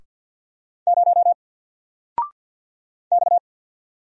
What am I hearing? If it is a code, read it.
Morse code, heard as a keyed tone, about 50 wpm, 700 Hz. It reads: 0 X
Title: Single Letter-Number 50wpm